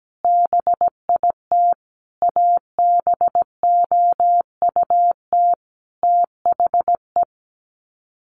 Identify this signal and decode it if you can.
Morse code, heard as a keyed tone, about 17 wpm, 700 Hz. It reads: BIT ABOUT THE
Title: Sets of 5 Words Encore2 17wpm